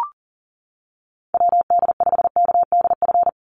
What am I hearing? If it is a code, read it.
Morse code, heard as a keyed tone, about 40 wpm, 700 Hz. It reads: WD5KDF
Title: USA Call Signs 40wpm